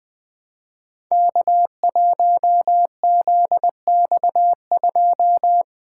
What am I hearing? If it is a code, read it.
Morse code, heard as a keyed tone, about 20 wpm, 700 Hz. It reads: K1ZX2
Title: Code Groups - 5 Letter:Numbers - 20wpm